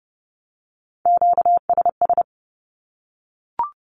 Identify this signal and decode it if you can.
Morse code, heard as a keyed tone, about 30 wpm, 700 Hz. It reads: QSS
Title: Q-Codes 30wpm